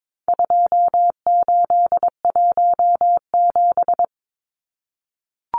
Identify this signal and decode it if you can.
Morse code, heard as a keyed tone, about 22 wpm, 700 Hz. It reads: 2817
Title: Numbers - 4-Digit 22wpm